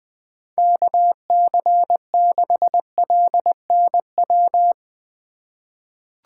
Morse code, heard as a keyed tone, about 20 wpm, 700 Hz. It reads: KC6LNW
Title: USA Call Signs - 6 Characters - 2x Character Spacing - 20wpm